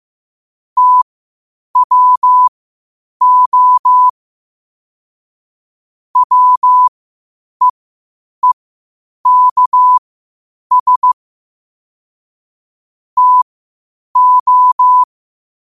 Morse code, heard as a keyed tone, about 15 wpm, spaced as for 8 wpm, 1000 Hz. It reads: TWO WEEKS TO
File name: qotdfiles / 8wpm